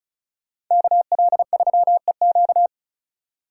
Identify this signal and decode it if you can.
Morse code, heard as a keyed tone, about 45 wpm, 700 Hz. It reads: KL3EQ